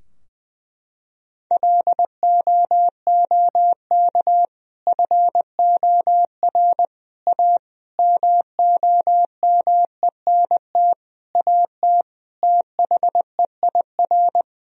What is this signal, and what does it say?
Morse code, heard as a keyed tone, about 21 wpm, 700 Hz. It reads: LOOK FOR A MOMENT AT THEIR
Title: Sentences from Top 1000 Words - Review Part 2 - 20wpm